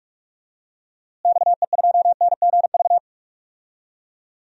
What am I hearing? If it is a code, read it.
Morse code, heard as a keyed tone, about 44 wpm, 700 Hz. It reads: XE2NGV
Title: DXCC-100 Call Signs - 6 Characters - Repeated 3x - 45wpm